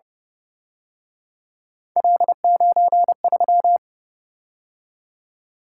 Morse code, heard as a keyed tone, about 29 wpm, 700 Hz. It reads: L93
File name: Speed-Racing - Three Letter-Number Combinations - 25wpm